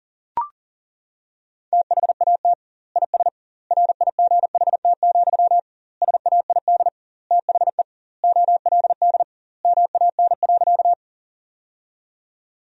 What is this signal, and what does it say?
Morse code, heard as a keyed tone, about 40 wpm, 700 Hz. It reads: THAT IS RIGHT, SAID THE OLD MAN.
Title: 2x Word-Spacing - Sentences from Top 200 Words - 40wpm